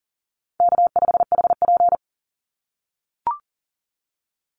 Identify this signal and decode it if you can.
Morse code, heard as a keyed tone, about 40 wpm, 700 Hz. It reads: K5HP
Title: Easy Call Signs - 40wpm